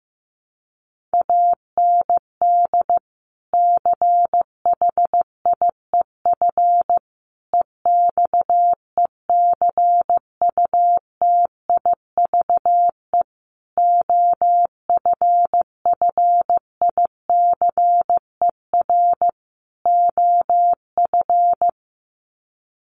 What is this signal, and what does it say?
Morse code, heard as a keyed tone, about 15 wpm, 700 Hz. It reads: AND CHIEF EXECUTIVE OFFICER OF
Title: Sets of 5 Words Encore2 15wpm